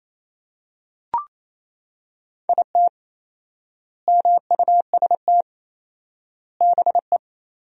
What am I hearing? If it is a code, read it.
Morse code, heard as a keyed tone, about 28 wpm, 700 Hz. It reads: IT MUST BE
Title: 4x Word-Spacing - Sentences from Top 500 Words - 28wpm